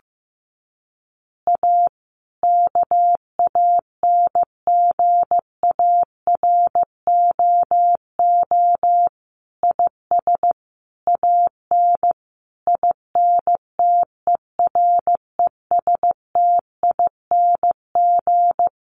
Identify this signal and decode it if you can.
Morse code, heard as a keyed tone, about 15 wpm, 700 Hz. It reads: A KANGAROO IS AN INTERESTING
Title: Easy Phrases 15wpm